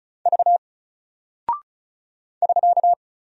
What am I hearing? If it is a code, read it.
Morse code, heard as a keyed tone, about 36 wpm, 700 Hz. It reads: V <SK>
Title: Single Character (Letter:Number:Punctuation:Prosign) Alt 35wpm